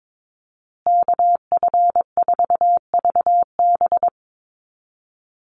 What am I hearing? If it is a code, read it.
Morse code, heard as a keyed tone, about 22 wpm, 700 Hz. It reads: KF4VB
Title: USA Call Signs Repeated 2x 22wpm